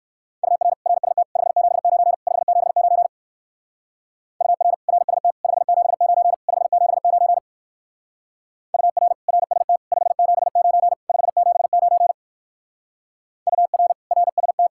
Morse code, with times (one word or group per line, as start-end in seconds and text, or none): 0.4-0.7 UR
0.9-1.2 RST
1.4-2.1 579
2.3-3.1 579
4.4-4.7 UR
4.9-5.3 RST
5.4-6.3 579
6.5-7.4 579
8.7-9.1 UR
9.3-9.8 RST
9.9-10.9 579
11.1-12.1 579
13.5-13.9 UR
14.1-14.7 RST